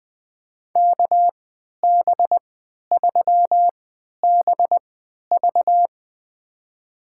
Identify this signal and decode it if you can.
Morse code, heard as a keyed tone, about 21 wpm, spaced as for 12 wpm, 700 Hz. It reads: KB3BV
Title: USA Call Signs - 5 Characters - 3x Character Spacing - 20wpm